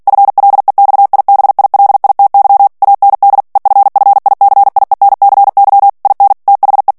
Morse code, heard as a keyed tone, about 48 wpm, 800 Hz. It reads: FLEXIBILITY AND EFFICIENCY IN THE
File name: ANGLAIS texte clair 3 vitesse 45 mots minute